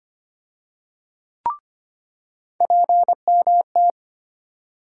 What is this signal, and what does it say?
Morse code, heard as a keyed tone, about 25 wpm, 700 Hz. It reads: PMT
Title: USA Call Sign Postfixes - 3 Letters 25wpm